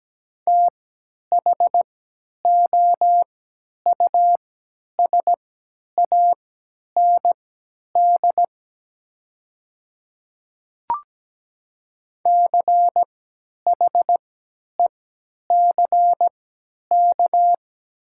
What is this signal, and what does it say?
Morse code, heard as a keyed tone, about 17 wpm, spaced as for 10 wpm, 700 Hz. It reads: THOUSAND CHECK
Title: 3x Character Spacing - Top 500 Words - 17wpm